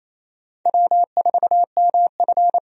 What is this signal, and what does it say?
Morse code, heard as a keyed tone, about 27 wpm, 700 Hz. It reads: W4MF
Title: USA Call Signs - 4 Characters - 28wpm